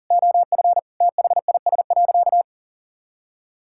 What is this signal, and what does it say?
Morse code, heard as a keyed tone, about 40 wpm, 700 Hz. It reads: OF THIS.
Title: Sentences from Top 1000 Words - Review Part 6 - 40wpm